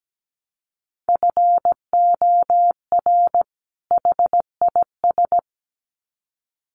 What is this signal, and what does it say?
Morse code, heard as a keyed tone, about 17 wpm, 700 Hz. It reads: FOR HIS
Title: Sets of 2 Words 17wpm